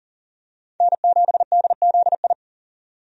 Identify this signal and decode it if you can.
Morse code, heard as a keyed tone, about 38 wpm, 700 Hz. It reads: N7DZI
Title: USA Call Signs - 5 Characters - 40wpm